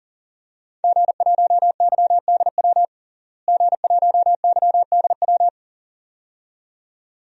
Morse code, heard as a keyed tone, about 40 wpm, 700 Hz. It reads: G1YDW G1YDW
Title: DXCC-100 Call Signs - 5 Characters - Repeated 2x - 40wpm